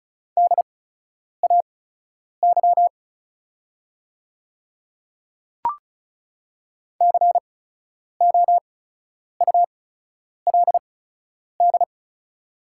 Morse code, heard as a keyed tone, about 36 wpm, spaced as for 10 wpm, 690 Hz. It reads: DAY COULD
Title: 8x Character Spacing - Top 100 Words - 35wpm